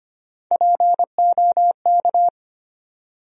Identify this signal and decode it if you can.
Morse code, heard as a keyed tone, about 25 wpm, 700 Hz. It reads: POK
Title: USA Call Sign Postfixes - 3 Letters 25wpm